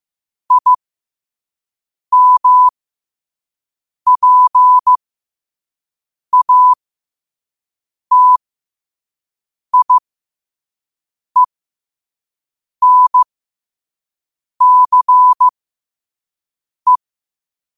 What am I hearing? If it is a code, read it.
Morse code, heard as a keyed tone, about 15 wpm, spaced as for 5 wpm, 1000 Hz. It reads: IMPATIENCE
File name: qotdfiles / 5wpm